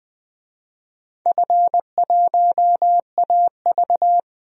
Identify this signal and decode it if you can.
Morse code, heard as a keyed tone, about 21 wpm, 700 Hz. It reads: F1AV
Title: DXCC-100 Call Signs - 4 Characters - Repeated 3x - 20wpm